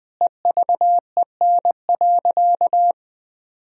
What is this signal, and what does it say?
Morse code, heard as a keyed tone, about 20 wpm, 700 Hz. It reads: EVEN.
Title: Sentences from Top 1000 Words - Review Part 1 - 20wpm